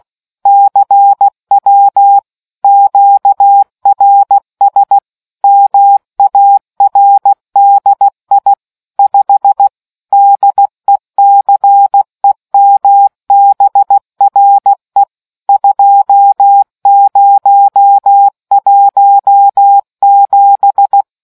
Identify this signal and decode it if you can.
Morse code, heard as a keyed tone, about 16 wpm, 800 Hz. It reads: CW QRS MARDI 5 DECEMBRE 2017